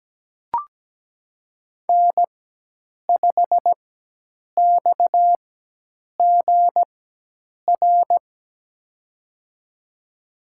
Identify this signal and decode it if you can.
Morse code, heard as a keyed tone, about 17 wpm, spaced as for 8 wpm, 700 Hz. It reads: N5XGR
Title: USA Call Signs - 5 Characters - 4x Character Spacing - 17wpm